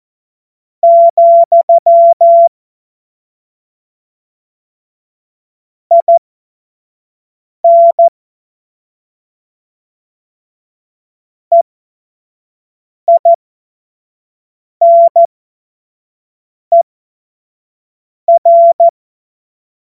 Morse code, heard as a keyed tone, about 14 wpm, spaced as for 5 wpm, 680 Hz. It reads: , IN EINER
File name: Morsekurs / Witze / Witz 16. Dezember 2025 25 F70